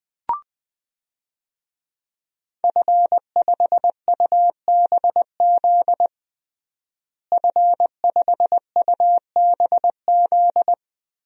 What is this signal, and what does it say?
Morse code, heard as a keyed tone, about 21 wpm, 700 Hz. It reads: F5UBZ F5UBZ
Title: DXCC-100 Call Signs - 5 Characters - Repeated 3x - 20wpm